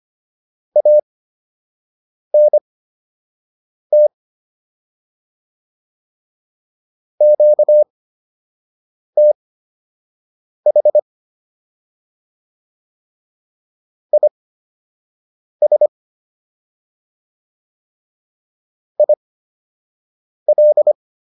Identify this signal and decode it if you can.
Morse code, heard as a keyed tone, about 25 wpm, spaced as for 6 wpm, 600 Hz. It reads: ANT QTH IS IL